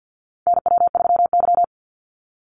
Morse code, heard as a keyed tone, about 50 wpm, 700 Hz. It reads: NW3Y
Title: Call Signs Repeated 2x - 50wpm